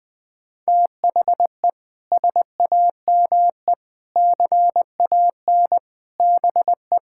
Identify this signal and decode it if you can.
Morse code, heard as a keyed tone, about 20 wpm, 700 Hz. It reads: THE SAME CAN BE
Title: Sets of 6 Words - 20wpm